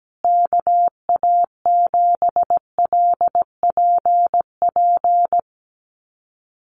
Morse code, heard as a keyed tone, about 17 wpm, 700 Hz. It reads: KA7LPP
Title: USA Call Signs Repeated 2x 17wpm